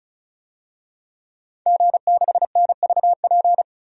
Morse code, heard as a keyed tone, about 34 wpm, 700 Hz. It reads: G6NVP
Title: DXCC-100 Call Signs - 5 Characters - 35wpm